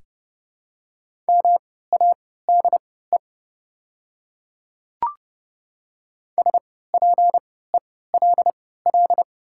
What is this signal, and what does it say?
Morse code, heard as a keyed tone, about 29 wpm, spaced as for 17 wpm, 690 Hz. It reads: MADE SPELL
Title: 3x Character Spacing - Top 300 Words - 30wpm